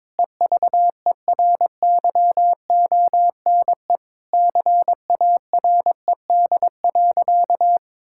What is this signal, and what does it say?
Morse code, heard as a keyed tone, about 22 wpm, 700 Hz. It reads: EVERYONE CARED.
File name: practice / 22wpm / Sentences with 8-Word Phrases - 22wpm